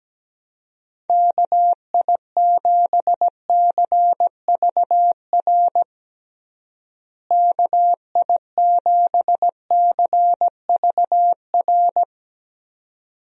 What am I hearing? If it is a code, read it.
Morse code, heard as a keyed tone, about 17 wpm, 700 Hz. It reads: KI7CVR KI7CVR
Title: USA Call Signs - 6 Characters - Repeated 2x - 17wpm